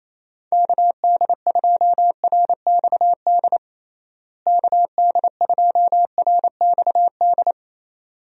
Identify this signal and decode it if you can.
Morse code, heard as a keyed tone, about 28 wpm, 700 Hz. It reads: KD2RXD KD2RXD
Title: USA Call Signs - 6 Characters - Repeated 2x - 28wpm